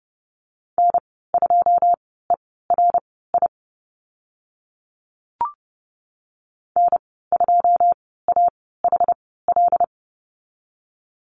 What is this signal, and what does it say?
Morse code, heard as a keyed tone, about 30 wpm, spaced as for 17 wpm, 700 Hz. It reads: N2ERI N2AHL
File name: Lesson 14g - Number 2 - Call Signs 30wpm